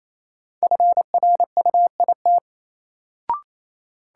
Morse code, heard as a keyed tone, about 27 wpm, 700 Hz. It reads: FRUIT
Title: Sentences from Top 1300 Words - New Word Drill - 28wpm